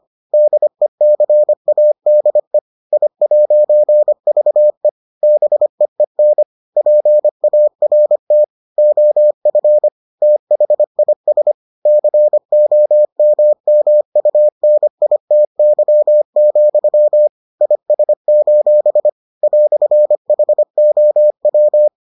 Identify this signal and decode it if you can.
Morse code, heard as a keyed tone, about 25 wpm, 600 Hz. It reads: DECADE I'VE BEEN PART OF THIS COMMUNITY, IS: "HOW